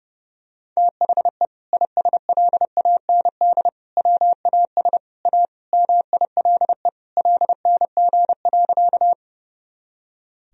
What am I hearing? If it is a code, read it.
Morse code, heard as a keyed tone, about 32 wpm, 700 Hz. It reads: THE ISLAND WAS A MILE LNG.
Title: Sentences from Top 500 Words and Abbreviations 30wpm